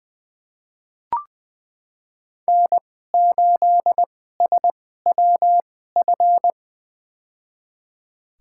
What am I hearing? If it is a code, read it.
Morse code, heard as a keyed tone, about 20 wpm, spaced as for 14 wpm, 700 Hz. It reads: N8SWF
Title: USA Call Signs - 2x Character Spacing - 20wpm